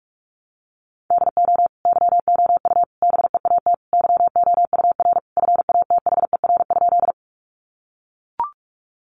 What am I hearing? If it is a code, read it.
Morse code, heard as a keyed tone, about 44 wpm, 700 Hz. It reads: DO YOU BEAT YOUR FATHER?
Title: Sentences from Top 1300 Words 45wpm